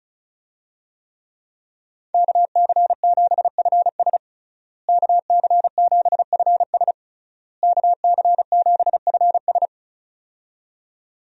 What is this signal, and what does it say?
Morse code, heard as a keyed tone, about 36 wpm, 700 Hz. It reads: KC7FS KC7FS KC7FS
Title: USA Call Signs - 5 Characters - Repeated 3x - 35wpm